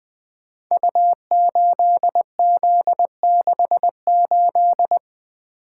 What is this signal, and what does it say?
Morse code, heard as a keyed tone, about 20 wpm, 700 Hz. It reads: U8Z68
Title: Code Groups - 5 Letter:Numbers - 20wpm